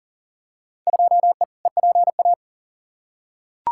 Morse code, heard as a keyed tone, about 40 wpm, 700 Hz. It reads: 2E EPA
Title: Field Day Exchanges Encore - 40wpm